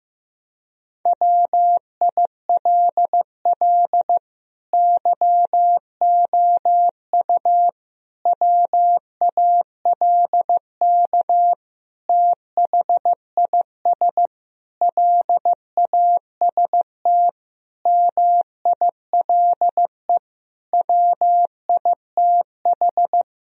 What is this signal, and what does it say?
Morse code, heard as a keyed tone, about 15 wpm, 700 Hz. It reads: WILL YOU WALK THIS LAST MILE WITH
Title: Sentences from Top 1000 Words - Review Part 1 - 15wpm